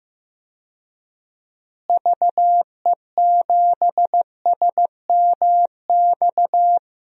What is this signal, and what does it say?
Morse code, heard as a keyed tone, about 15 wpm, 700 Hz. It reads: VE7SMX
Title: DXCC-100 Call Signs - 6 Characters - Repeated 2x - 15wpm